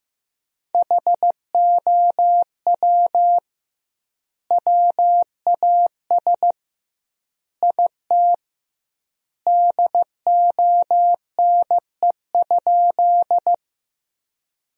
Morse code, heard as a keyed tone, about 15 wpm, 700 Hz. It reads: HOW WAS IT DONE?
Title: 2x Word-Spacing - Sentences from Top 500 Words - 15wpm